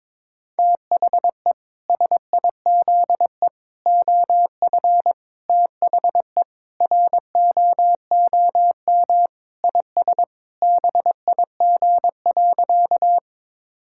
Morse code, heard as a keyed tone, about 22 wpm, 700 Hz. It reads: THE SIZE OF THE ROOM IS BIG.
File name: practice / 22wpm / Sentences with 3-Word Phrases - 22wpm